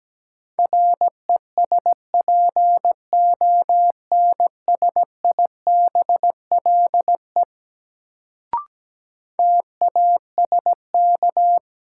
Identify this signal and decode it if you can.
Morse code, heard as a keyed tone, about 17 wpm, 700 Hz. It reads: RESPONSIBLE TASK
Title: Sentences from Top 1000 Words - Review Part 7 - New Word Drill - 17wpm